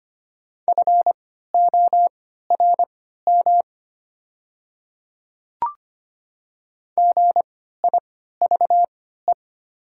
Morse code, heard as a keyed tone, about 25 wpm, spaced as for 14 wpm, 700 Hz. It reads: FORM GIVE
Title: 3x Character Spacing - Top 300 Words - 25wpm